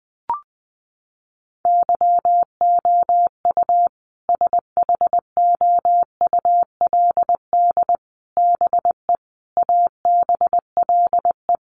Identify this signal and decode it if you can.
Morse code, heard as a keyed tone, about 20 wpm, 700 Hz. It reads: YOU SHOULD BE ABLE
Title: Sets of 5 Words Encore 20wpm